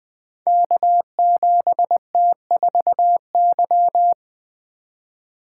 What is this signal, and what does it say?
Morse code, heard as a keyed tone, about 20 wpm, 700 Hz. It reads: K7T4Y
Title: Code Groups - 5 Letter:Numbers - 20wpm